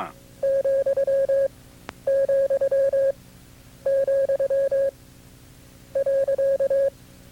Morse code, heard as a keyed tone, about 22 wpm, spaced as for 10 wpm, 550 Hz. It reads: ,,, .
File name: CW-practice/KB6MT-tape2-side1